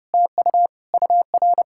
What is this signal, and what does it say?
Morse code, heard as a keyed tone, about 33 wpm, 700 Hz. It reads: TU UR